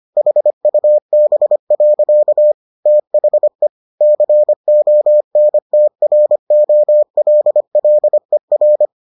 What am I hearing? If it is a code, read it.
Morse code, heard as a keyed tone, about 25 wpm, 600 Hz. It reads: HUB. THE CONTROLLER